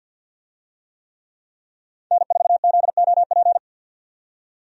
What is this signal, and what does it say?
Morse code, heard as a keyed tone, about 50 wpm, 700 Hz. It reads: N4ZOP
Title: USA Call Signs - 5 Characters - 50wpm